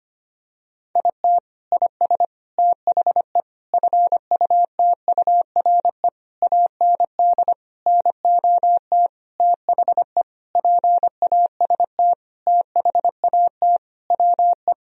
Morse code, heard as a keyed tone, about 25 wpm, 700 Hz. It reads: IT IS THE FUTURE AND NOT THE PAST THAT WE
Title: Sentences from Top 1000 Words - Review Part 4 - 25wpm